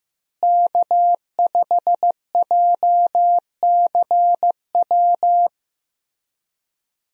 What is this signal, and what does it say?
Morse code, heard as a keyed tone, about 15 wpm, 700 Hz. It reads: K5JCW
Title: USA Call Signs - 5 Characters - Repeated 3x - 15wpm